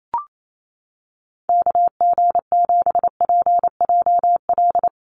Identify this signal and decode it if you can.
Morse code, heard as a keyed tone, about 28 wpm, 700 Hz. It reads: KG7PJL
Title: USA Call Signs Repeated 3x 28wpm